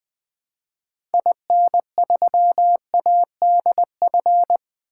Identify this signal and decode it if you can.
Morse code, heard as a keyed tone, about 20 wpm, 700 Hz. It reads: IN3ADF
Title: 2023 SOTA Activator Call Signs - 20wpm